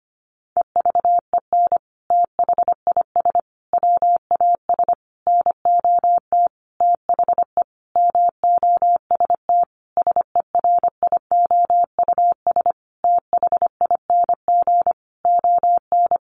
Morse code, heard as a keyed tone, about 25 wpm, 700 Hz. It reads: EVEN THIS WAS NOT THE MOST SERIOUS THING ON
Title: Sentences from Top 1100 Words 25wpm